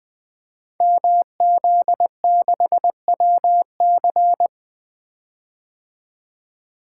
Morse code, heard as a keyed tone, about 20 wpm, 700 Hz. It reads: MZ6WC
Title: Code Groups - 5 Letter:Numbers - 20wpm